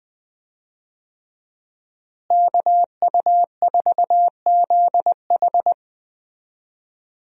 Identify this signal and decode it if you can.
Morse code, heard as a keyed tone, about 20 wpm, 700 Hz. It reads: KU4ZH
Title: USA Call Signs - 5 Characters - Repeated 3x - 20wpm